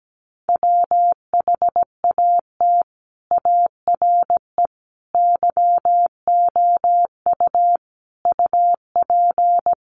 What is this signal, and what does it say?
Morse code, heard as a keyed tone, about 17 wpm, 700 Hz. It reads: WHAT ARE YOU UP
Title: Sentences from Top 100 Words 17wpm